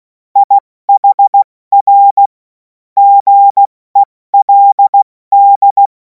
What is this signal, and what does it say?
Morse code, heard as a keyed tone, about 16 wpm, 800 Hz. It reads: IHR GELD
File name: Morsekurs / Witze / Witz 03. Februar 2026 70 F80